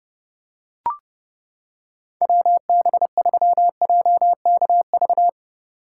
Morse code, heard as a keyed tone, about 29 wpm, 700 Hz. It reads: WB3JKV